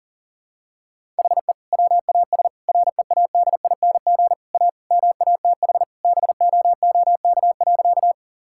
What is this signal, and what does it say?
Morse code, heard as a keyed tone, about 40 wpm, 700 Hz. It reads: HE WAS READING A MATH BOOK.
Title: Sentences from Top 1000 Words - Review Part 6 - 40wpm